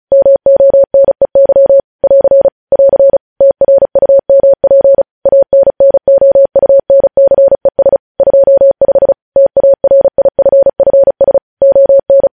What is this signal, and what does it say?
Morse code, heard as a keyed tone, about 36 wpm, 550 Hz. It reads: MONEY <AR> <AR> TRUMP ANNOUNCES 25 TARIFFS ON
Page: News Headlines in Morse Code at 35 WPM